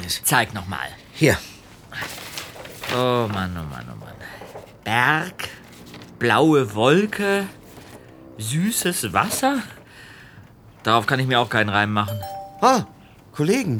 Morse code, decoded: none